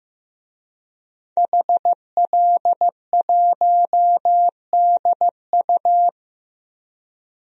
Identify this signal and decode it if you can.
Morse code, heard as a keyed tone, about 15 wpm, 700 Hz. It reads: HL1DU
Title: DXCC-100 Call Signs - 5 Characters - Repeated 2x - 15wpm